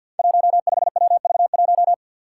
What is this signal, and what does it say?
Morse code, heard as a keyed tone, about 51 wpm, 700 Hz. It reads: 15WV1
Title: Five Challenging Letter-Number Combinations 50wpm